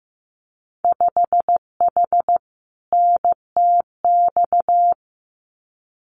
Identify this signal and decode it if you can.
Morse code, heard as a keyed tone, about 15 wpm, 700 Hz. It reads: 5H NTX
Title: Winter Field Day Exchanges Encore 15wpm